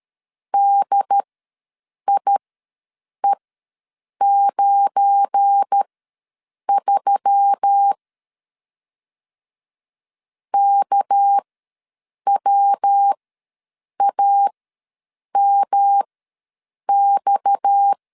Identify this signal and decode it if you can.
Morse code, heard as a keyed tone, about 12 wpm, spaced as for 7 wpm, 800 Hz. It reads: DIE93 KWAMX
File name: LEZIONI / 8 YJ45 / 8 YJ4 ws 12wpm 3249